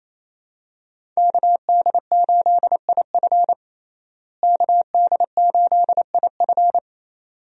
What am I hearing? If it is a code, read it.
Morse code, heard as a keyed tone, about 28 wpm, 700 Hz. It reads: KD8IF KD8IF
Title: USA Call Signs - 5 Characters - Repeated 2x - 28wpm